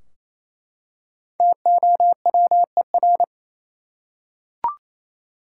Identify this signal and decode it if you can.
Morse code, heard as a keyed tone, about 28 wpm, 700 Hz. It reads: TOWER
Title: Five Letter Words - 28wpm